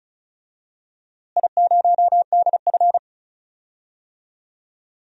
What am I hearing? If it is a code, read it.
Morse code, heard as a keyed tone, about 36 wpm, 700 Hz. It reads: I0DF
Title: DXCC-100 Call Signs - 4 Characters - Repeated 3x - 35wpm